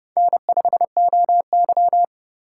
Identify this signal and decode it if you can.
Morse code, heard as a keyed tone, about 32 wpm, 700 Hz. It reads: N5OY
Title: USA Call Signs - 3x Character Spacing - 30wpm